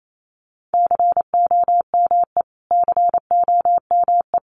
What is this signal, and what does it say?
Morse code, heard as a keyed tone, about 28 wpm, 700 Hz. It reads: COME COME
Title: Top 500 Words Repeated 28wpm